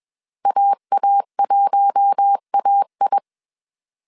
Morse code, heard as a keyed tone, about 21 wpm, 800 Hz. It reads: AA1AI
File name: CW / DET / 1 9 1500C S C 20wpm 2310